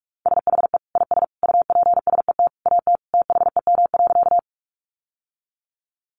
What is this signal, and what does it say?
Morse code, heard as a keyed tone, about 44 wpm, 700 Hz. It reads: SHE IS UPSET AT THEM.
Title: Sentences from Top 800 Words 45wpm